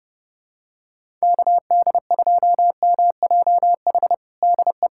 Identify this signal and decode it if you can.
Morse code, heard as a keyed tone, about 30 wpm, 700 Hz. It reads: KD2MJH DE
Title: USA Callsign DE Callsign 30wpm